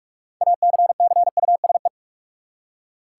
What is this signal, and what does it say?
Morse code, heard as a keyed tone, about 41 wpm, 700 Hz. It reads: ACCUSE
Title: Six Letter Words - 45wpm